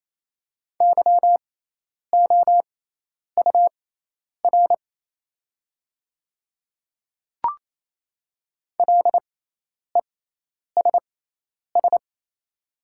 Morse code, heard as a keyed tone, about 27 wpm, spaced as for 10 wpm, 700 Hz. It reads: YOUR LESS
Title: 6x Character Spacing - Top 500 Words - 28wpm